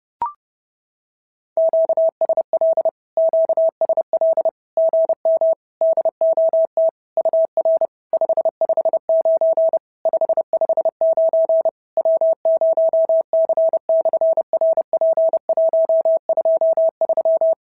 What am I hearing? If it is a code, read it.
Morse code, heard as a keyed tone, about 30 wpm, 650 Hz. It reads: QSL QSL GM DOT UR 559 559 W0C/RP123
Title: SOTA QSO Practice 30wpm